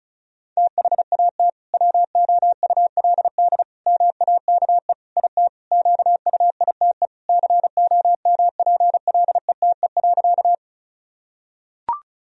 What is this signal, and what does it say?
Morse code, heard as a keyed tone, about 36 wpm, 700 Hz. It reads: THAT WOULD MAKE IT QUITE COMPLETE.
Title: Sentences from Top 1000 Words - Review Part 2 - 35wpm